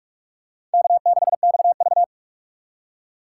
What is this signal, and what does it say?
Morse code, heard as a keyed tone, about 47 wpm, 700 Hz. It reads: K6XV
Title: USA Call Signs - 4 Characters - 2x Character Spacing - 45wpm